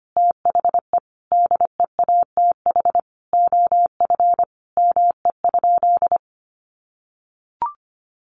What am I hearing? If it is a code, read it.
Morse code, heard as a keyed tone, about 25 wpm, 700 Hz. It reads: THE DEATH OF ME?
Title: Sentences from Top 1100 Words 25wpm